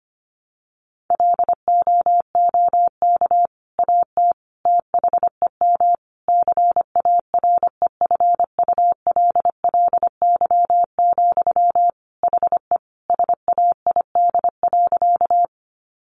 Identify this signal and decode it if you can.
Morse code, heard as a keyed tone, about 25 wpm, 700 Hz. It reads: LOOK AT THEM CAREFULLY, HE SAID.
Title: Sentences from Top 1300 Words 25wpm